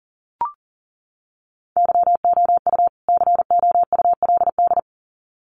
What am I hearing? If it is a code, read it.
Morse code, heard as a keyed tone, about 40 wpm, 700 Hz. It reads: YOU COULD
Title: Sets of 2 Words 40wpm